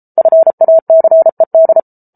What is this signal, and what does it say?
Morse code, heard as a keyed tone, about 34 wpm, 650 Hz. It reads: FACED